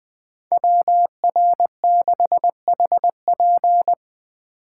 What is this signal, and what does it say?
Morse code, heard as a keyed tone, about 20 wpm, 700 Hz. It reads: WR6HP
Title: USA Call Signs - 5 Characters - 4x Character Spacing - 20wpm